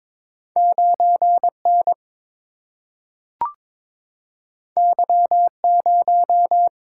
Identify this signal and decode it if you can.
Morse code, heard as a keyed tone, about 22 wpm, 700 Hz. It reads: 9N Y0
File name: practice / 22wpm / Two Letter-Number Combinations - 22wpm